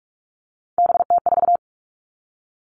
Morse code, heard as a keyed tone, about 44 wpm, 700 Hz. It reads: BT4
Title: All Cut Numbers - 3-Digit 45wpm